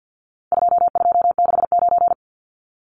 Morse code, heard as a keyed tone, about 50 wpm, 700 Hz. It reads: 2269
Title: Numbers - 4-Digit 50wpm